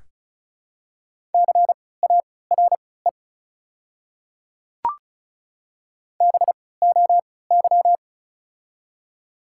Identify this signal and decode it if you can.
Morse code, heard as a keyed tone, about 34 wpm, spaced as for 20 wpm, 700 Hz. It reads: CARE BOY
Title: 3x Character Spacing - Top 500 Words - 35wpm